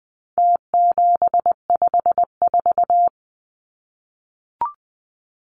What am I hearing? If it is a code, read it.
Morse code, heard as a keyed tone, about 20 wpm, 700 Hz. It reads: T754
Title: Numbers - Contest Style - 4-Digit 20wpm